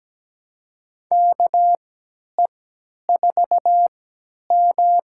Morse code, heard as a keyed tone, about 17 wpm, spaced as for 10 wpm, 700 Hz. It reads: KE4M
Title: USA Call Signs - 3x Character Spacing - 17wpm